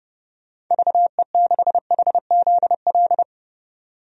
Morse code, heard as a keyed tone, about 31 wpm, 700 Hz. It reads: VE6HZL
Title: DXCC-100 Call Signs - 6 Characters - 30wpm